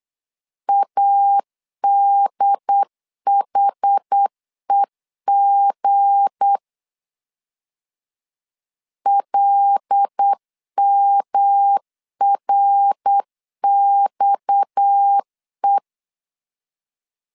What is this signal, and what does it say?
Morse code, heard as a keyed tone, about 8 wpm, 800 Hz. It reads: ADHEG LMRXE
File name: LEZIONI / 8 YJ45 / 8 YJ 8wpm 3313